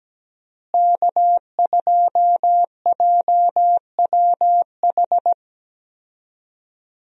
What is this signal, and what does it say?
Morse code, heard as a keyed tone, about 17 wpm, 700 Hz. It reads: K2JWH
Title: USA Call Signs - 5 Characters - Repeated 2x - 17wpm